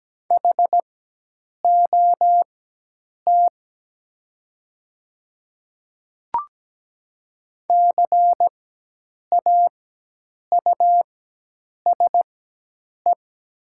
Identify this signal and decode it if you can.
Morse code, heard as a keyed tone, about 17 wpm, spaced as for 8 wpm, 700 Hz. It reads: HOT CAUSE
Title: 4x Character Spacing - Top 200 Words - 17wpm